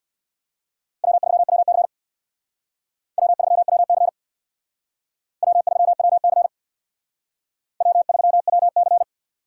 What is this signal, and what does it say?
Morse code, heard as a keyed tone, about 62 wpm, 700 Hz. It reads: W3WC W3WC W3WC W3WC
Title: Speed Racing - USA 1x2 Call Signs 50wpm